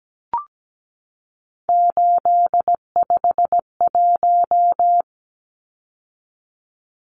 Morse code, heard as a keyed tone, about 17 wpm, 700 Hz. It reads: 851
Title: Numbers - 3-Digit 17wpm